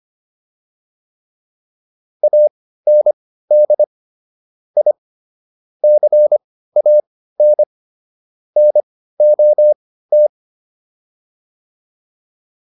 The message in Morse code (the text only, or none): AND I CAN NOT